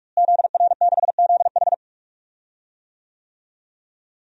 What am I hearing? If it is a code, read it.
Morse code, heard as a keyed tone, about 42 wpm, 700 Hz. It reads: ZR6ZH
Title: DXCC-100 Call Signs - 5 Characters - Repeated 3x - 45wpm